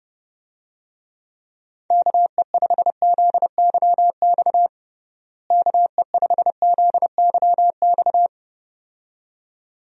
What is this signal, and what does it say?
Morse code, heard as a keyed tone, about 30 wpm, 700 Hz. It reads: KE5ZYX KE5ZYX
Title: USA Call Signs - 6 Characters - Repeated 2x - 30wpm